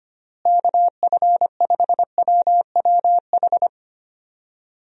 Morse code, heard as a keyed tone, about 25 wpm, 700 Hz. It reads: KF5WWH
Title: USA Call Signs - 6 Characters - Repeated 2x - 25wpm